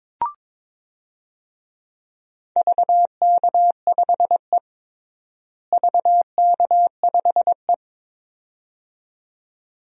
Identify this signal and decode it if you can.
Morse code, heard as a keyed tone, about 21 wpm, 700 Hz. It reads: VK5E VK5E
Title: DXCC-100 Call Signs - 4 Characters - Repeated 2x - 22wpm